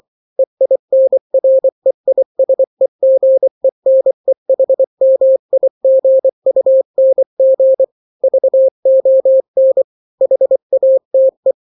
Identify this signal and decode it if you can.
Morse code, heard as a keyed tone, about 24 wpm, 530 Hz. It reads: EINREISEGENEHMIGUNG VON HATE
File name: daily / 23wpm